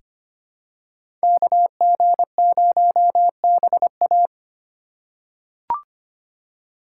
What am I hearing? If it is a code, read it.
Morse code, heard as a keyed tone, about 25 wpm, 700 Hz. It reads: KG0BA
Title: USA Call Signs - 2x Character Spacing - 25wpm